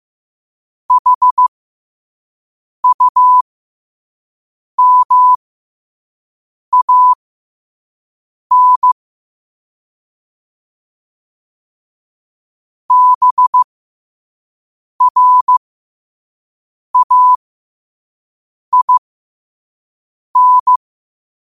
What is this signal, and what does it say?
Morse code, heard as a keyed tone, about 15 wpm, spaced as for 5 wpm, 1000 Hz. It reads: HUMAN BRAIN
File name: qotdfiles / 5wpm